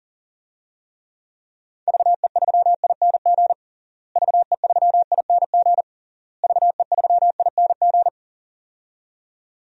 Morse code, heard as a keyed tone, about 40 wpm, 700 Hz. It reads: VE3ING VE3ING VE3ING
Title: DXCC-100 Call Signs - 6 Characters - Repeated 3x - 40wpm